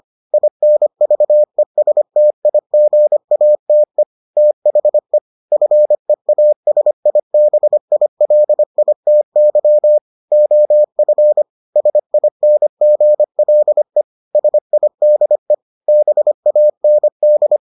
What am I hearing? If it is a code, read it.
Morse code, heard as a keyed tone, about 25 wpm, 600 Hz. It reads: INVESTIGATE THE FEASIBILITY OF SINGLE SIDE BAND